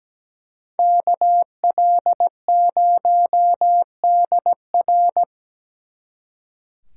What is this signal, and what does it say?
Morse code, heard as a keyed tone, about 20 wpm, 700 Hz. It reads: KL0DR
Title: Speed Racing - USA 2x2 Call Signs 17wpm